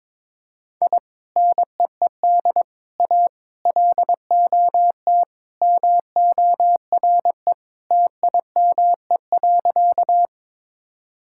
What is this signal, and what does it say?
Morse code, heard as a keyed tone, about 22 wpm, 700 Hz. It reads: I NEED A LOT MORE TIME.
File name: practice / 22wpm / Sentences with 3-Word Phrases - 22wpm